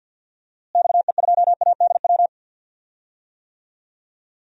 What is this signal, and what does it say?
Morse code, heard as a keyed tone, about 52 wpm, 700 Hz. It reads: XE2ADW
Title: DXCC-100 Call Signs - 6 Characters - Repeated 3x - 50wpm